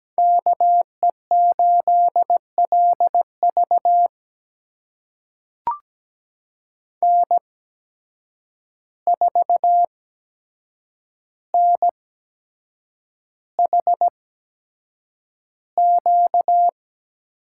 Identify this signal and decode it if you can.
Morse code, heard as a keyed tone, about 17 wpm, spaced as for 7 wpm, 700 Hz. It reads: KE8LV N4NHQ
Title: USA Call Signs - 5 Characters - 8x Character Spacing - 17wpm